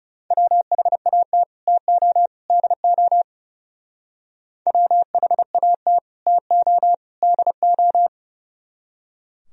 Morse code, heard as a keyed tone, about 38 wpm, 700 Hz. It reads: WHAT TO DO WHAT TO DO
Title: Speed-Racing - Sets of 3 Words 30wpm